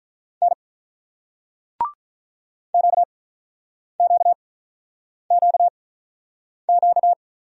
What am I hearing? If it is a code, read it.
Morse code, heard as a keyed tone, about 49 wpm, spaced as for 9 wpm, 700 Hz. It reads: N QQQQ